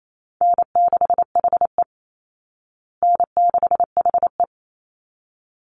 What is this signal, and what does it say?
Morse code, heard as a keyed tone, about 28 wpm, 700 Hz. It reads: N6HE N6HE
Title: Call Signs Repeated 3x - 28wpm